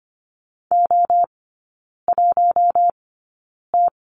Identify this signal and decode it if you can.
Morse code, heard as a keyed tone, about 25 wpm, spaced as for 9 wpm, 700 Hz. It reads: O1T
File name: course / 25:10wpm / 093 Lesson 16e - Letter c - Three Characters 25wpm